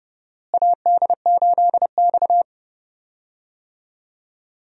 Morse code, heard as a keyed tone, about 32 wpm, 700 Hz. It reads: AD8X
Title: USA Call Signs - 4 Characters - Repeated 3x - 30wpm